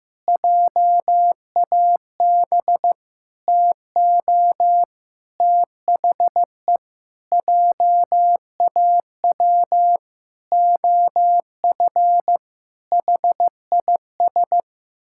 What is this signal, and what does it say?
Morse code, heard as a keyed tone, about 15 wpm, 700 Hz. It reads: JAB TO THE JAW OF HIS
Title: Pangram Challenge v2 - 15wpm